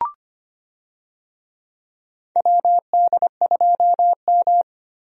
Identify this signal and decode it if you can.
Morse code, heard as a keyed tone, about 25 wpm, 700 Hz. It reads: WD2M
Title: USA Call Signs - 4 Characters - 25wpm